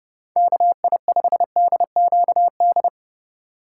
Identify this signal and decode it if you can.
Morse code, heard as a keyed tone, about 37 wpm, 700 Hz. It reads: KI5DQD
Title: Speed Racing - USA Call Signs 30wpm